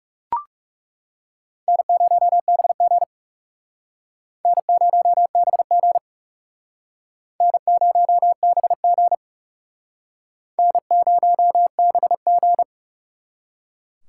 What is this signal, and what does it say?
Morse code, heard as a keyed tone, about 37 wpm, 700 Hz. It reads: N0BG N0BG N0BG N0BG
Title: Speed Racing - USA 1x2 Call Signs 30wpm